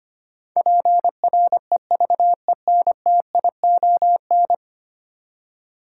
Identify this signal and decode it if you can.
Morse code, heard as a keyed tone, about 25 wpm, 700 Hz. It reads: PREVENTION
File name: 10 Letter Words - First 3 Letters Given - 25wpm